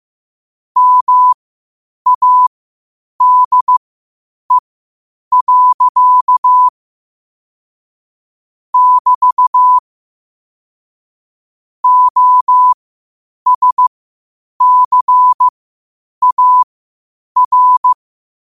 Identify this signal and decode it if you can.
Morse code, heard as a keyed tone, about 15 wpm, spaced as for 8 wpm, 1000 Hz. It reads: MADE. <BT> OSCAR
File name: qotdfiles / 8wpm